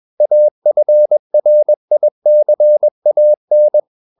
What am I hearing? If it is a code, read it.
Morse code, heard as a keyed tone, about 21 wpm, 600 Hz. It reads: AFRICAN
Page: CW med Gnister